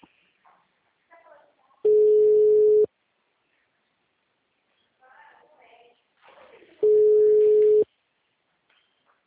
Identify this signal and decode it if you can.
Ringback tone (Europe, 425 Hz), with speech.